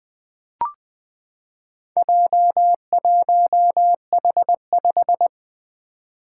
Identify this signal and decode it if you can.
Morse code, heard as a keyed tone, about 20 wpm, 700 Hz. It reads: J1H5
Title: Four Letter-Number Combinations - 20wpm